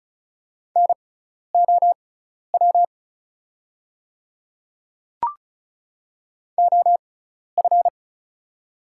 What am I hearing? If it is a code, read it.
Morse code, heard as a keyed tone, about 36 wpm, spaced as for 12 wpm, 700 Hz. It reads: NOW OF
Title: 6x Character Spacing - Top 300 Words - 35wpm